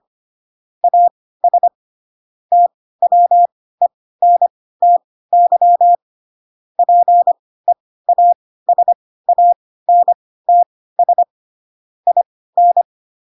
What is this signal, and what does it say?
Morse code, heard as a keyed tone, about 25 wpm, spaced as for 16 wpm, 700 Hz. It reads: AS TWENTY PEASANTS IN
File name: Pinocchio / Pinocchio ch3 10 25 ik1hge